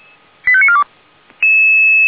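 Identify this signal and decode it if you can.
Five-tone selective call (ZVEI), heard as quick runs of five tones, tones of about 70 ms. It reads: 85942
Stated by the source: ZVEI five-tone selective call 89467, received by SDR.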